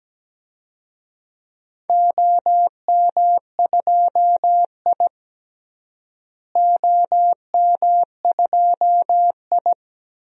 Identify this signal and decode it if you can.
Morse code, heard as a keyed tone, about 17 wpm, 700 Hz. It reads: OM2I OM2I
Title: DXCC-100 Call Signs - 4 Characters - Repeated 3x - 17wpm